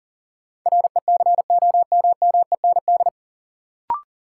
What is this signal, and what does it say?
Morse code, heard as a keyed tone, about 40 wpm, 700 Hz. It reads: RECOMMEND
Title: Sentences from Top 1200 Words - New Word Drill - 40wpm